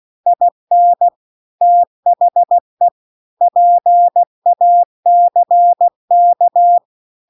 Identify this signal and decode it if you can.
Morse code, heard as a keyed tone, about 16 wpm, 700 Hz. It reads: IN THE PACK